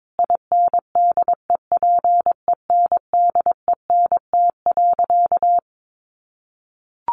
Morse code, heard as a keyed tone, about 22 wpm, 700 Hz. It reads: INDEPENDENT.
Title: Sentences from Top 900 Words 22wpm